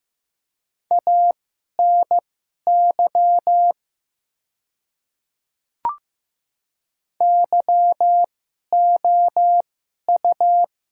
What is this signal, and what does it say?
Morse code, heard as a keyed tone, about 15 wpm, spaced as for 11 wpm, 700 Hz. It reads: ANY YOU
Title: 2x Character Spacing - Top 200 Words - 15wpm